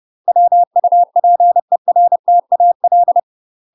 Morse code, heard as a keyed tone, about 30 wpm, 700 Hz. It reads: WUPERTAL
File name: UK MASTER QSOs TRAINING / 30WPM / UK MASTER QSOs 0018